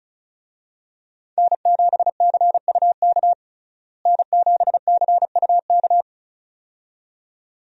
Morse code, heard as a keyed tone, about 34 wpm, 700 Hz. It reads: N7CUK N7CUK
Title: USA Call Signs - 5 Characters - Repeated 2x - 35wpm